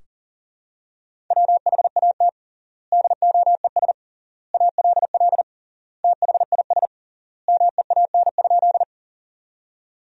Morse code, heard as a keyed tone, about 37 wpm, 700 Hz. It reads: WHAT DOES ALL THIS MEAN?
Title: 3x Word-Spacing - Sentences from Top 200 Words - 40wpm